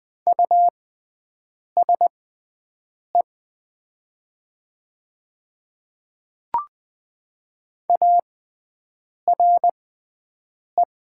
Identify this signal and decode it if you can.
Morse code, heard as a keyed tone, about 20 wpm, spaced as for 7 wpm, 700 Hz. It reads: USE ARE
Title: 6x Character Spacing - Top 200 Words - 20wpm